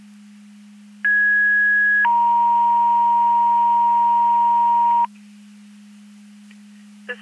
Two-tone page, heard as a one-second tone, then a three-second tone, about 1650 then 940 Hz. It is heard from 1.0 to 5.0 s.
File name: Portage 154.385 Paging